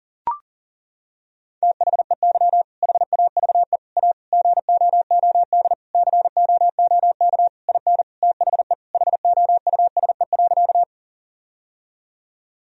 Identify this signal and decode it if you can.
Morse code, heard as a keyed tone, about 40 wpm, 700 Hz. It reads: THEY HAVE A GOOD COOK IN THE HOUSE.
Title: Sentences from Top 1000 Words - Review Part 4 - 40wpm